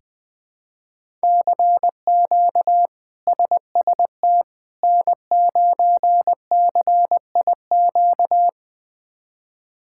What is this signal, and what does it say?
Morse code, heard as a keyed tone, about 20 wpm, 700 Hz. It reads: CQ SST N9CIQ
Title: K1USN SST QSO Practice - 20wpm